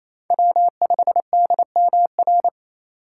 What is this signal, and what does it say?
Morse code, heard as a keyed tone, about 28 wpm, 690 Hz. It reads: W5DMR
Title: USA Call Signs - 2x Character Spacing - 28wpm